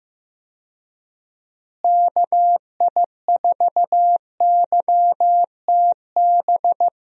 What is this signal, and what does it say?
Morse code, heard as a keyed tone, about 16 wpm, 700 Hz. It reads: KI4YTB
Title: USA Call Signs - 6 Characters - Repeated 2x - 15wpm